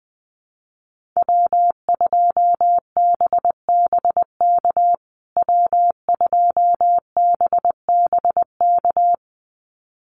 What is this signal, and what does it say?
Morse code, heard as a keyed tone, about 20 wpm, 700 Hz. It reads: W2BBK W2BBK
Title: USA Call Signs Repeated 2x 20wpm